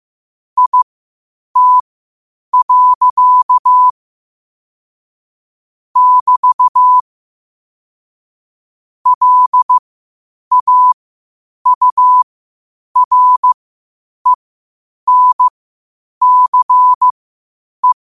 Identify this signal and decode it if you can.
Morse code, heard as a keyed tone, about 15 wpm, spaced as for 8 wpm, 1000 Hz. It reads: IT. <BT> LAURENCE
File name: qotdfiles / 8wpm